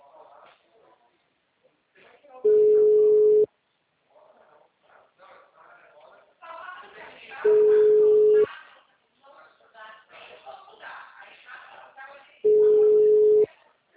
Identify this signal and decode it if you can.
Ringback tone (Europe, 425 Hz), with speech.